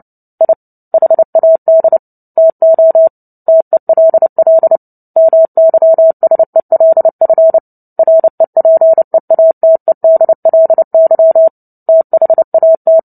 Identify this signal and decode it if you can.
Morse code, heard as a keyed tone, about 30 wpm, 650 Hz. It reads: I HAD TO TELL MYSELF REPEATEDLY THAT
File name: Book Two / 03-Chapter 3 - 29 WPM